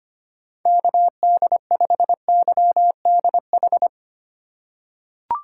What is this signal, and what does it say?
Morse code, heard as a keyed tone, about 25 wpm, 700 Hz. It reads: KD5YDH
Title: USA Call Signs - 6 Characters - 3x Character Spacing - 25wpm